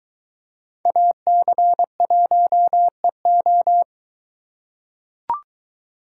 Morse code, heard as a keyed tone, about 23 wpm, 700 Hz. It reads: AC1EO